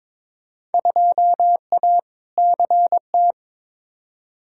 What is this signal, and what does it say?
Morse code, heard as a keyed tone, about 22 wpm, 700 Hz. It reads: 2A CT
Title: Field Day Exchanges Encore - 22wpm